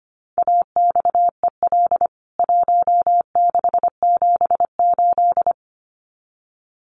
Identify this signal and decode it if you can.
Morse code, heard as a keyed tone, about 25 wpm, 700 Hz. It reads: AXEL 1678
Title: CWT Exchange 25wpm